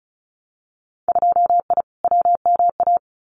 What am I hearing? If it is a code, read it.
Morse code, heard as a keyed tone, about 34 wpm, 700 Hz. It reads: 2I WMA
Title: Winter Field Day Exchanges Encore 35wpm